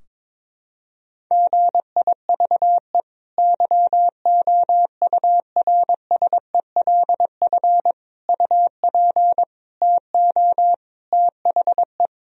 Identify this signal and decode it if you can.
Morse code, heard as a keyed tone, about 22 wpm, 700 Hz. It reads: GIVE YOURSELF UP TO THE
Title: Sentences from Top 1000 Words - Review Part 4 - 22wpm